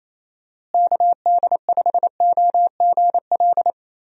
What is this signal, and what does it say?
Morse code, heard as a keyed tone, about 28 wpm, 700 Hz. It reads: KD5OGL
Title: USA Call Signs - 6 Characters - Repeated 3x - 28wpm